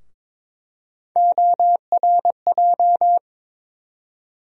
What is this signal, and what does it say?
Morse code, heard as a keyed tone, about 22 wpm, 700 Hz. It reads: ORJ
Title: USA Call Sign Postfixes - 3 Letters 22wpm